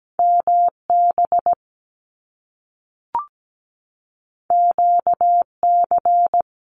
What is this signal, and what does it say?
Morse code, heard as a keyed tone, about 17 wpm, 700 Hz. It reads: MB QC
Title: Canadian Provinces and Territories 17wpm